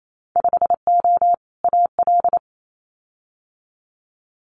Morse code, heard as a keyed tone, about 28 wpm, 700 Hz. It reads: 5O AL
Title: Winter Field Day Exchanges 28wpm